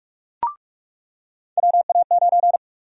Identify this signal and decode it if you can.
Morse code, heard as a keyed tone, about 41 wpm, 700 Hz. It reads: WA9
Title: USA Call Sign Prefixes - 45wpm